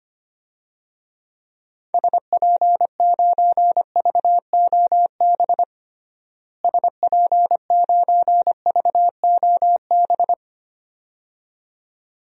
Morse code, heard as a keyed tone, about 25 wpm, 700 Hz. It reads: SP9VOB SP9VOB
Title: DXCC-100 Call Signs - 6 Characters - Repeated 2x - 25wpm